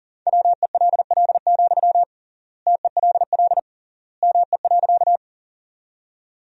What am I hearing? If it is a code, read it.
Morse code, heard as a keyed tone, about 40 wpm, 700 Hz. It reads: WELL, TELL ME.
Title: 3x Word-Spacing - Sentences from Top 200 Words - 40wpm